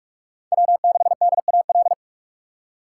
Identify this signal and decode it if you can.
Morse code, heard as a keyed tone, about 41 wpm, 700 Hz. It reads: W6DAL